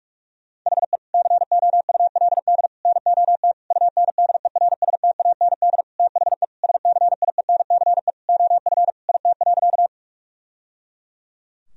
Morse code, heard as a keyed tone, about 46 wpm, 700 Hz. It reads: HE COULD NOT UNDERSTAND THE SCIENCE OF IT.
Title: Sentences from Top 1000 Words - Review Part 2 - 45wpm